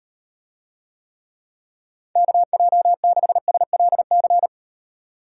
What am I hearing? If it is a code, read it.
Morse code, heard as a keyed tone, about 38 wpm, 700 Hz. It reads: KJ6SLC